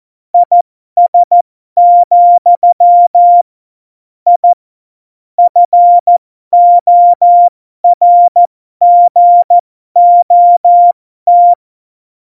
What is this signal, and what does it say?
Morse code, heard as a keyed tone, about 14 wpm, spaced as for 12 wpm, 700 Hz. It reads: IS, I FORGOT